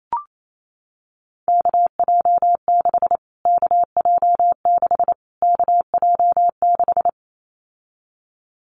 Morse code, heard as a keyed tone, about 28 wpm, 700 Hz. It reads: KJ6 KJ6 KJ6
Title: Callsign Prefixes for Top 25 DXCC - Repeated 3x - 28wpm